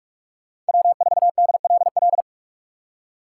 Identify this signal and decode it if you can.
Morse code, heard as a keyed tone, about 59 wpm, spaced as for 51 wpm, 700 Hz. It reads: W4DLL
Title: Speed Racing - USA Call Signs 45wpm